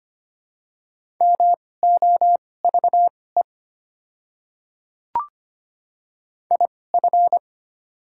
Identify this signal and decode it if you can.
Morse code, heard as a keyed tone, about 25 wpm, spaced as for 18 wpm, 690 Hz. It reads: MOVE IF
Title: 2x Character Spacing - Top 300 Words - 25wpm